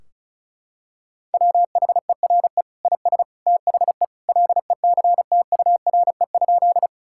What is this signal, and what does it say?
Morse code, heard as a keyed tone, about 36 wpm, 700 Hz. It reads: WHERE IS THE LECTURE?
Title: Sentences from Top 1600 Words 35wpm